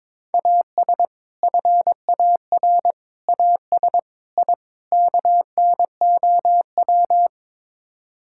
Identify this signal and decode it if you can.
Morse code, heard as a keyed tone, about 22 wpm, 700 Hz. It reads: AS FAR AS I KNOW
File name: practice / 22wpm / Build A Phrase - 5-Word Phrases - 22wpm